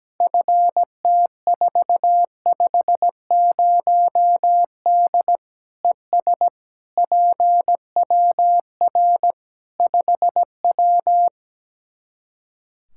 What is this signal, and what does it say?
Morse code, heard as a keyed tone, about 17 wpm, 700 Hz. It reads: FT450D ES PWR 5W
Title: Practice QSO - Exchange 2 - Info - 17wpm